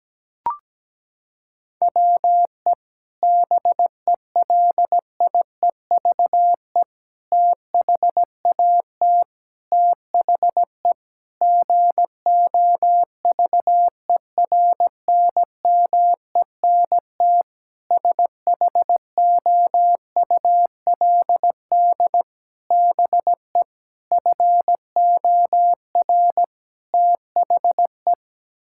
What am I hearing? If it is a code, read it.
Morse code, heard as a keyed tone, about 17 wpm, 700 Hz. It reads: WE BELIEVE THAT THE GOVERNMENT SHOULD BE FOR THE
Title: Sentences from Top 1000 Words - Review Part 6 - 17wpm